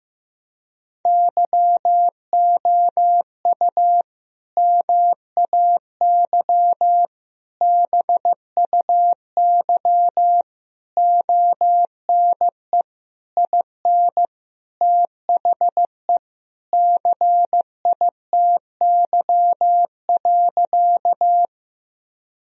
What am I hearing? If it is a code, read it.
Morse code, heard as a keyed tone, about 16 wpm, 700 Hz. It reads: YOU MAY BUY ONE IN THE CITY.
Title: Sentences from Top 1000 Words - Review Part 4 - 15wpm